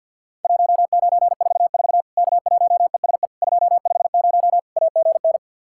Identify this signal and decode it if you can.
Morse code, heard as a keyed tone, about 52 wpm, 700 Hz. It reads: 1944 K1ESE 250 AGN
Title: Worked All Europe DX Contest - QTC Exchange with Repeats - 50wpm